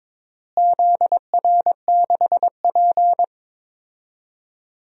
Morse code, heard as a keyed tone, about 22 wpm, 700 Hz. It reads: ZR6P
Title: DXCC-100 Call Signs - 4 Characters - Repeated 3x - 22wpm